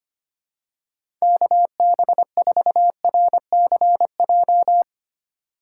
Morse code, heard as a keyed tone, about 25 wpm, 700 Hz. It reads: KB4RCJ
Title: USA Call Signs - 6 Characters - 6x Character Spacing - 25wpm